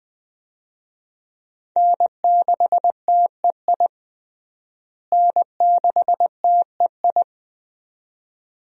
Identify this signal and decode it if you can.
Morse code, heard as a keyed tone, about 20 wpm, 700 Hz. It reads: N6TEI N6TEI
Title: USA Call Signs - 5 Characters - Repeated 2x - 20wpm